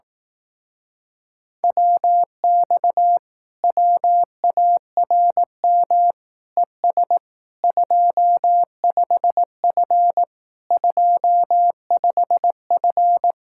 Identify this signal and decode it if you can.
Morse code, heard as a keyed tone, about 18 wpm, 700 Hz. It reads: WX WARM ES 25F 25F
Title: Speed-Racing Common QSO Phrases Encore - 15wpm